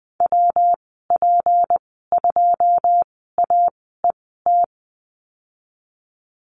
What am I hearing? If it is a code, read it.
Morse code, heard as a keyed tone, about 20 wpm, spaced as for 15 wpm, 700 Hz. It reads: WP2AET
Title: 186 Lesson 30g - Letter b - Call Signs 20wpm